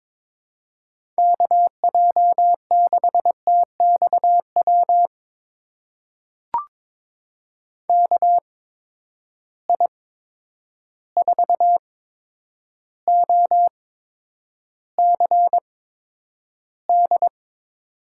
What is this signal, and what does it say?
Morse code, heard as a keyed tone, about 22 wpm, spaced as for 9 wpm, 700 Hz. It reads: KJ6TXW KI4OCD
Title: USA Call Signs - 6 Characters - 8x Character Spacing - 22wpm